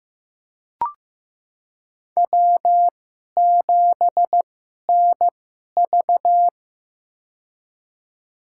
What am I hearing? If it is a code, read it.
Morse code, heard as a keyed tone, about 15 wpm, spaced as for 11 wpm, 700 Hz. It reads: W7NV
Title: USA Call Signs - 4 Characters - 2x Character Spacing - 15wpm